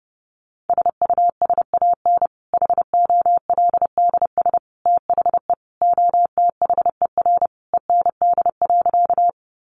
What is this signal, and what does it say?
Morse code, heard as a keyed tone, about 30 wpm, 700 Hz. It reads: SUSAN HOLDS THE OTHER END.
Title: Easy Phrases 30wpm